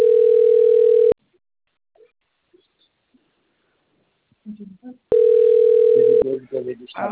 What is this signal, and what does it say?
Ringback tone (North America), 440 + 480 Hz, with speech.
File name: out- - - - - .44377